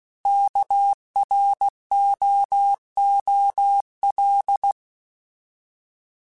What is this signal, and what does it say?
Morse code, heard as a keyed tone, about 16 wpm, 800 Hz. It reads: KROOL
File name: Alphabetical / Alphabetical mp3s / Ks @15